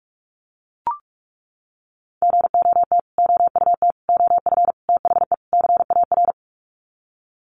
Morse code, heard as a keyed tone, about 44 wpm, 700 Hz. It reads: GOT OUT OF THE CAR
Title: Sets of 5 Words Encore 45wpm